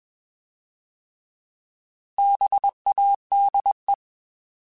Morse code, heard as a keyed tone, about 21 wpm, 800 Hz. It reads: BADE
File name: Bs @20